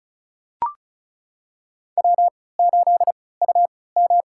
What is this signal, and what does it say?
Morse code, heard as a keyed tone, about 36 wpm, spaced as for 20 wpm, 700 Hz. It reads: W8UM
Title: USA Call Signs - 4 Characters - 3x Character Spacing - 35wpm